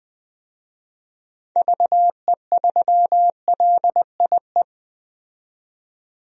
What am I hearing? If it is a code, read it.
Morse code, heard as a keyed tone, about 21 wpm, 700 Hz. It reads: VE3LIE
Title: DXCC-100 Call Signs - 6 Characters - 20wpm